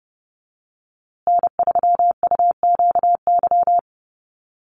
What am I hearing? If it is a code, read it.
Morse code, heard as a keyed tone, about 30 wpm, 700 Hz. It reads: N3UQY
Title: USA Call Signs Repeated 3x 30wpm